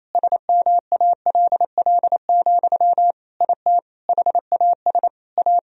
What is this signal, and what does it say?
Morse code, heard as a keyed tone, about 27 wpm, 700 Hz. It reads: SMALL, IT HAS A
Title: Sentences from Top 1000 Words - Review Part 3 - 28wpm